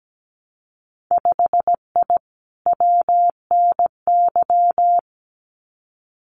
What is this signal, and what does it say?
Morse code, heard as a keyed tone, about 17 wpm, 700 Hz. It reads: 5I WNY
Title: Winter Field Day Exchanges Encore 17wpm